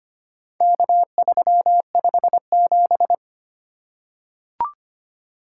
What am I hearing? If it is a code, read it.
Morse code, heard as a keyed tone, about 25 wpm, 700 Hz. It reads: K357
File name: practice / 25wpm / Four Letter-Number Combinations - 25wpm